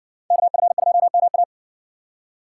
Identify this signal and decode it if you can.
Morse code, heard as a keyed tone, about 58 wpm, 700 Hz. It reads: KF2MA